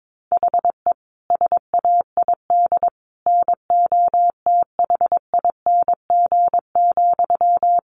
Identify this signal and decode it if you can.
Morse code, heard as a keyed tone, about 22 wpm, 700 Hz. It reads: HE SAID NOTHING,
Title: Sentences from Top 1500 Words 22wpm